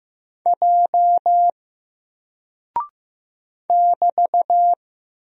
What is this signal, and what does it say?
Morse code, heard as a keyed tone, about 15 wpm, 700 Hz. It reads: J <BT>
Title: Single Character (Letter:Number:Punctuation:Prosign) Alt 15wpm